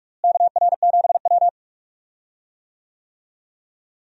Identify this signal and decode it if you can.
Morse code, heard as a keyed tone, about 46 wpm, 700 Hz. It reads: KR7W
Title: USA Call Signs - 4 Characters - Repeated 3x - 45wpm